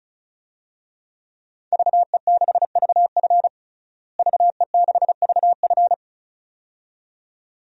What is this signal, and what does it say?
Morse code, heard as a keyed tone, about 36 wpm, 700 Hz. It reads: VE6VF VE6VF
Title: DXCC-100 Call Signs - 5 Characters - Repeated 2x - 35wpm